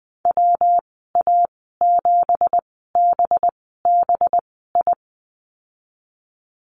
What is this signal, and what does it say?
Morse code, heard as a keyed tone, about 20 wpm, spaced as for 15 wpm, 700 Hz. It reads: WA7BBI
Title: 193 Lesson 31g - Letter v - Call Signs 20wpm